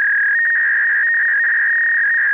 Fax machine handshake.